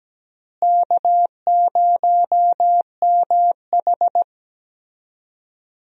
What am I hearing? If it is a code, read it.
Morse code, heard as a keyed tone, about 17 wpm, 700 Hz. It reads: K0MH
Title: USA Call Signs - 4 Characters - Repeated 3x - 17wpm